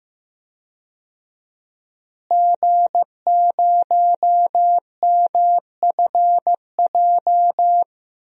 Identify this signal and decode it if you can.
Morse code, heard as a keyed tone, about 15 wpm, 700 Hz. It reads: G0MFJ
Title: DXCC-100 Call Signs - 5 Characters - Repeated 2x - 15wpm